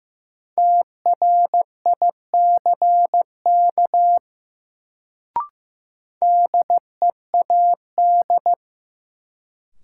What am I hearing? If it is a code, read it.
Morse code, heard as a keyed tone, about 15 wpm, 700 Hz. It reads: TRICK DEAD
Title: Sentences from Top 1500 Words - New Word Drill - 15wpm